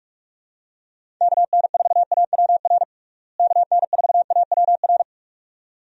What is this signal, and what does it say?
Morse code, heard as a keyed tone, about 46 wpm, 700 Hz. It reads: KN4AWR KN4AWR
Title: USA Call Signs - 6 Characters - Repeated 2x - 45wpm